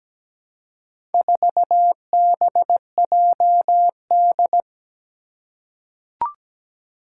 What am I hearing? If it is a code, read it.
Morse code, heard as a keyed tone, about 17 wpm, 700 Hz. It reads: 4BJD
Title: Four Challenging Letter-Number Combinations 17wpm